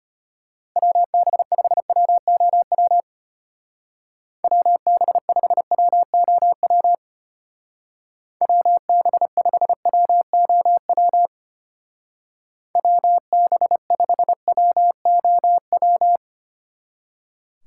Morse code, heard as a keyed tone, about 32 wpm, 700 Hz. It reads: WB5WOW WB5WOW WB5WOW WB5WOW